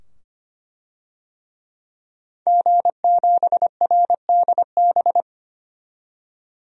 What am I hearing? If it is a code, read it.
Morse code, heard as a keyed tone, about 25 wpm, 700 Hz. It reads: G7RDB
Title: DXCC-100 Call Signs - 5 Characters - 25wpm